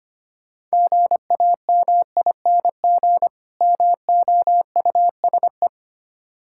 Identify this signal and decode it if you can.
Morse code, heard as a keyed tone, about 25 wpm, 700 Hz. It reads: GAMING MOUSE
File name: practice / 25wpm / Difficult Words with Hints - 25wpm